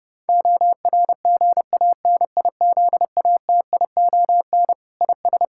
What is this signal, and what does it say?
Morse code, heard as a keyed tone, about 32 wpm, 700 Hz. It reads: ORGANIZATION IS
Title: Sentences from Top 1000 Words - Review Part 5 - 30wpm